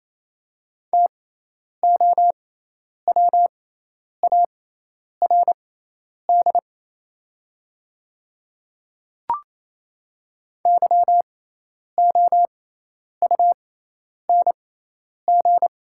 Morse code, heard as a keyed tone, about 28 wpm, spaced as for 10 wpm, 700 Hz. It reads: TOWARD YOUNG
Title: 6x Character Spacing - Top 500 Words - 28wpm